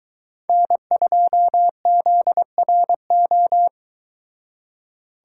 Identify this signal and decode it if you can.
Morse code, heard as a keyed tone, about 23 wpm, 700 Hz. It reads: N2ZRO